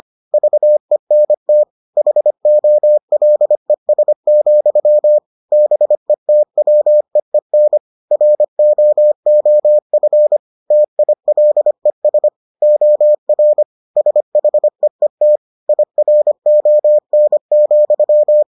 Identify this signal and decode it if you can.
Morse code, heard as a keyed tone, about 25 wpm, 600 Hz. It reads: VENT HOLES, BETWEEN ROOF TILES OR SHEET IRON,